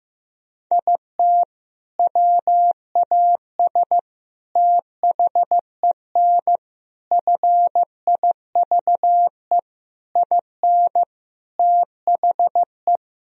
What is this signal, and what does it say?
Morse code, heard as a keyed tone, about 16 wpm, 700 Hz. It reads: IT WAS THEN FIVE IN THE
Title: Sentences from Top 1000 Words - Review Part 2 - 15wpm